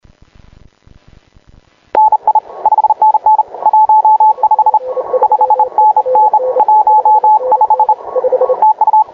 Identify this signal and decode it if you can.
Morse code, heard as a keyed tone, about 31 wpm, 860 Hz. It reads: NI 5NN 15 5NN 15 TU
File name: klub / cqww k8dd